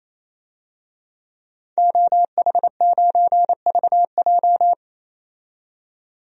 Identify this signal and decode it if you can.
Morse code, heard as a keyed tone, about 28 wpm, 700 Hz. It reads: OH9VJ
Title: DXCC-100 Call Signs - 5 Characters - 28wpm